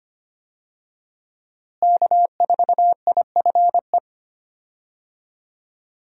Morse code, heard as a keyed tone, about 25 wpm, 700 Hz. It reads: K4IFE
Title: USA Call Signs - 5 Characters - Repeated 2x - 25wpm